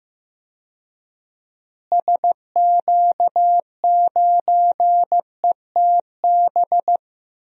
Morse code, heard as a keyed tone, about 15 wpm, 700 Hz. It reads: SQ9ETB
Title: DXCC-100 Call Signs - 6 Characters - Repeated 2x - 15wpm